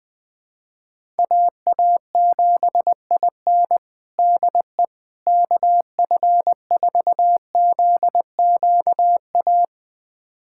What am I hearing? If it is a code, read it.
Morse code, heard as a keyed tone, about 20 wpm, 700 Hz. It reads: AA7IN DE KF4ZQA
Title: USA Callsign DE Callsign 20wpm